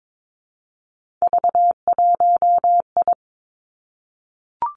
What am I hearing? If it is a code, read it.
Morse code, heard as a keyed tone, about 22 wpm, 700 Hz. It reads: V1I
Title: Three Challenging Letter-Number Combinations 22wpm